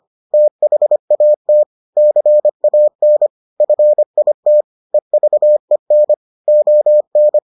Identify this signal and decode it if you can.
Morse code, heard as a keyed tone, about 25 wpm, 600 Hz. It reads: THAT CAN FIT EVEN ON